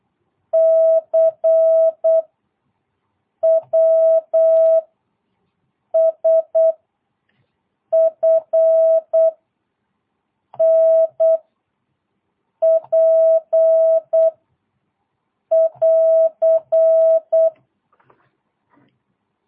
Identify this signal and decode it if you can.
Morse code, heard as a keyed tone, about 8 wpm, 660 Hz. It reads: C W S F N P <AR>
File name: jh8cbh / 2022cw